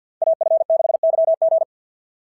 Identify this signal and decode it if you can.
Morse code, heard as a keyed tone, about 50 wpm, 690 Hz. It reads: AF6YG
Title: SOTA QSO Practice 50wpm